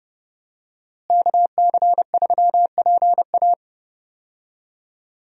Morse code, heard as a keyed tone, about 32 wpm, 700 Hz. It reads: KC3PA
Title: USA Call Signs - 5 Characters - Repeated 3x - 30wpm